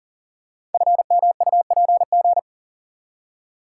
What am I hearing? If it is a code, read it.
Morse code, heard as a keyed tone, about 41 wpm, 700 Hz. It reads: FMUPG
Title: Code Groups - 5 Letters - 40wpm